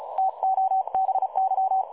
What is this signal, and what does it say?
Morse code, heard as a keyed tone, about 36 wpm, 760 Hz. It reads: TO BY
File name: dlazaawansowanych / nagraniazpasma / websdr recording 3565.0kHz